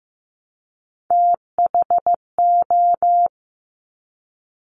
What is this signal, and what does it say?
Morse code, heard as a keyed tone, about 15 wpm, 700 Hz. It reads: THO
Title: Top Three Letter Combinations in Words 15wpm